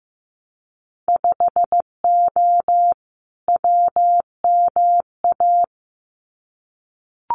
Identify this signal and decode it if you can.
Morse code, heard as a keyed tone, about 15 wpm, 700 Hz. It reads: 5O WMA
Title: Winter Field Day Exchanges 15wpm